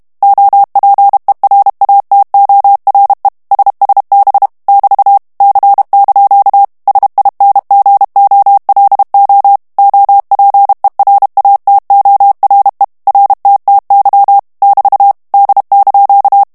Morse code, heard as a keyed tone, about 32 wpm, 800 Hz. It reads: OPERATORE SSB <BT> C) SINGOLO OPERATORE RTTY <BT> D)
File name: texte clair 2 vitesse 30 mots minute